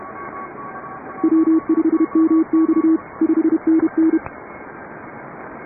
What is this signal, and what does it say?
Morse code, heard as a keyed tone, about 31 wpm, 310 Hz. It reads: W5MX 5NN
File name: chatham / 0216-15CW